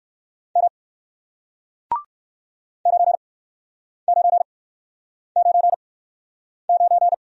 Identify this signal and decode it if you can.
Morse code, heard as a keyed tone, about 55 wpm, spaced as for 9 wpm, 700 Hz. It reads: M 9999
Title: Speed-Racing Single Letter-Number 45wpm